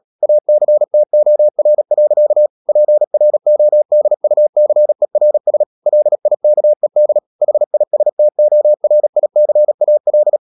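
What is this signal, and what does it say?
Morse code, heard as a keyed tone, about 37 wpm, 600 Hz. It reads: ACTOR. PRODUCERS LIKED HISTORICAL
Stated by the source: Morse code practice at 35–39 wpm sending English text of Ken Follett's novel Never, from CW med Gnister.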